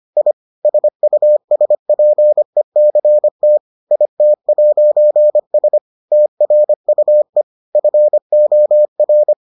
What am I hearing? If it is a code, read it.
Morse code, heard as a keyed tone, about 25 wpm, 600 Hz. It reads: I SUSPECT IT'S TRUE FOR